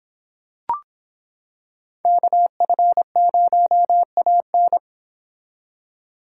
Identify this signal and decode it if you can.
Morse code, heard as a keyed tone, about 26 wpm, 700 Hz. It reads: KF0AN